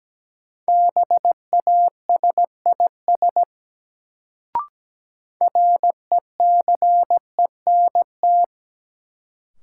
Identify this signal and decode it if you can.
Morse code, heard as a keyed tone, about 17 wpm, 700 Hz. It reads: BASIS RECENT
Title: Sets of 3 Words - New Word Drill - 17wpm